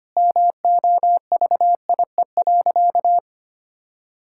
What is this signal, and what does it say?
Morse code, heard as a keyed tone, about 25 wpm, 700 Hz. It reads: MOVIE.
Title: Sentences from Top 1000 Words - Review Part 5 - 25wpm